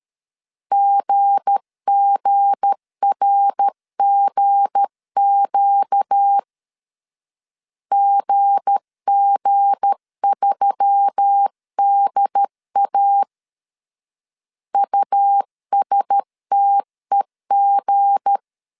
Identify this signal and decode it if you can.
Morse code, heard as a keyed tone, about 13 wpm, 800 Hz. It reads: GGRGQ GG3DA USTEG
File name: LEZIONI / 7 GF9Z / 7 G 12wpm 2055